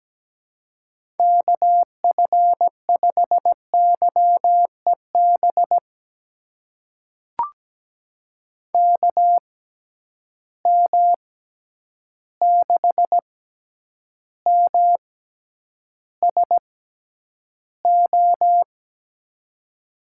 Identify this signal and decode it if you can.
Morse code, heard as a keyed tone, about 17 wpm, spaced as for 9 wpm, 700 Hz. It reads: KF5YEB KM6MSO
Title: USA Call Signs - 6 Characters - 6x Character Spacing - 17wpm